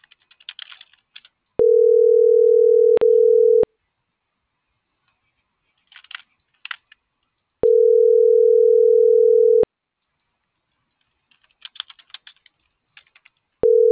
Ringback tone (North America), 440 + 480 Hz.